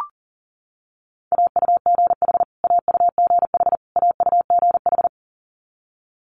Morse code, heard as a keyed tone, about 37 wpm, 700 Hz. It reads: AUGH AUGH AUGH
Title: Multi-letter Phonemes Repeated 3x 40wpm